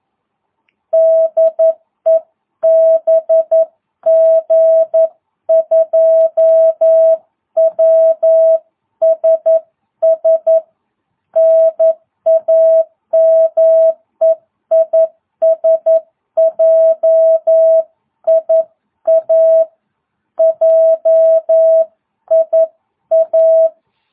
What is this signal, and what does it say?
Morse code, heard as a keyed tone, about 11 wpm, 650 Hz. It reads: DE BG2WSS NAME IS JIA JIA